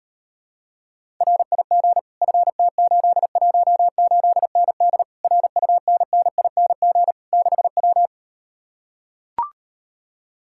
Morse code, heard as a keyed tone, about 39 wpm, 700 Hz. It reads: RIG FT818ND RUNNING 6W